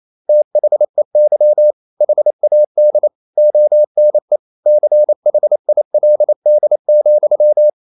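Morse code, heard as a keyed tone, about 28 wpm, 600 Hz. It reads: THEY HAD ONE CHILD,
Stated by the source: Morse code practice at 28 wpm, text of Ken Follett's novel Never in English, from CW med Gnister.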